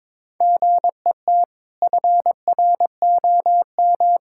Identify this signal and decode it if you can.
Morse code, heard as a keyed tone, about 22 wpm, 700 Hz. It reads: GET FROM
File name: practice / 22wpm / Sets of 7 Words - 22wpm